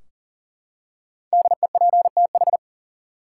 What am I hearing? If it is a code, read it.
Morse code, heard as a keyed tone, about 40 wpm, 700 Hz. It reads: DEPTH
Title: Five Letter Words - 40wpm